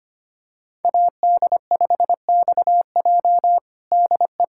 Morse code, heard as a keyed tone, about 25 wpm, 700 Hz. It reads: AD5XJ DE
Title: USA Callsign DE Callsign 25wpm